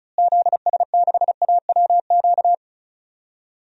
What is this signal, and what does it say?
Morse code, heard as a keyed tone, about 36 wpm, 700 Hz. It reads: ZS6AWQ
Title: DXCC-100 Call Signs - 6 Characters - Repeated 3x - 35wpm